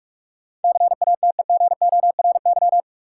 Morse code, heard as a keyed tone, about 46 wpm, 700 Hz. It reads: CATEGORY
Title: Sentences from Top 1000 Words - Review Part 7 - New Word Drill - 45wpm